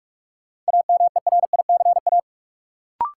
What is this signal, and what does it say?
Morse code, heard as a keyed tone, about 43 wpm, 700 Hz. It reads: AMERICA
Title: Sets of 5 Words - New Word Drill - 45wpm